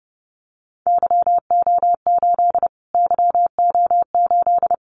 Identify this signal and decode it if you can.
Morse code, heard as a keyed tone, about 30 wpm, 700 Hz. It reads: YO8 YO8
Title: Callsign Prefixes for Top 25 DXCC - Repeated 3x - 30wpm